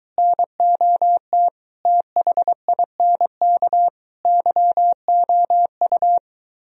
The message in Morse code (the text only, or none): NOT THINK YOU